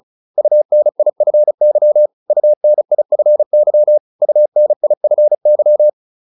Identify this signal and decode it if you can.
Morse code, heard as a keyed tone, about 34 wpm, 600 Hz. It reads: UNIFY UNIFY UNIFY
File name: UN5 35